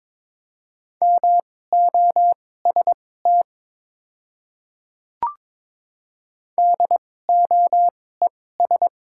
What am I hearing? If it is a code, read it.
Morse code, heard as a keyed tone, about 22 wpm, spaced as for 16 wpm, 700 Hz. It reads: MOST DOES
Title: 2x Character Spacing - Top 200 Words - 22wpm